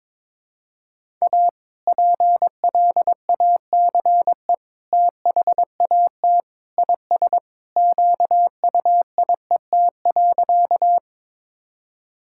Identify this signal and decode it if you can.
Morse code, heard as a keyed tone, about 22 wpm, 700 Hz. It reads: A PLACE THAT IS QUIET.
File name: practice / 22wpm / Sentences with 2-Word Phrases - 22wpm